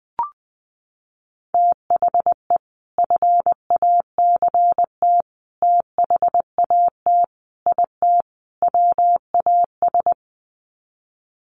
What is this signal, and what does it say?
Morse code, heard as a keyed tone, about 20 wpm, 700 Hz. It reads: THE FACT THAT IT WAS
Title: Sets of 5 Words Encore2 20wpm